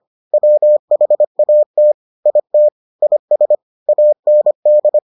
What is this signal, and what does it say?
Morse code, heard as a keyed tone, about 25 wpm, 600 Hz. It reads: WHAT IT IS AND